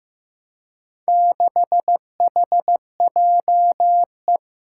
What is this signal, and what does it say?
Morse code, heard as a keyed tone, about 15 wpm, 700 Hz. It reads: 6HJE
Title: Four Challenging Letter-Number Combinations 15wpm